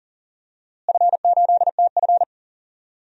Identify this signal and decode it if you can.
Morse code, heard as a keyed tone, about 40 wpm, 700 Hz. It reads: F8TF
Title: DXCC-100 Call Signs - 4 Characters - Repeated 3x - 40wpm